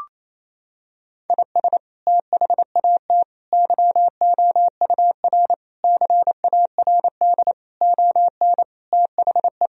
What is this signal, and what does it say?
Morse code, heard as a keyed tone, about 28 wpm, 700 Hz. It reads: IS THAT YOUR CARD ON THE
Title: Sentences from Top 1000 Words - Review Part 4 - 28wpm